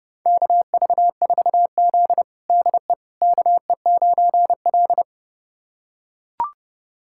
Morse code, heard as a keyed tone, about 30 wpm, 700 Hz. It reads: KV4Z DE KE9L
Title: USA Callsign DE Callsign - 4 Characters - 30wpm